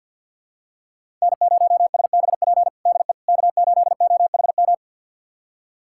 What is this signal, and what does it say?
Morse code, heard as a keyed tone, about 52 wpm, 690 Hz. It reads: N0SBP DE K8OHM
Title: USA Callsign DE Callsign - 5 Characters - 50wpm